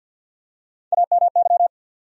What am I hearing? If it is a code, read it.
Morse code, heard as a keyed tone, about 52 wpm, 700 Hz. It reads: AMY
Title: USA Common Names - Max 5 Letters - 50wpm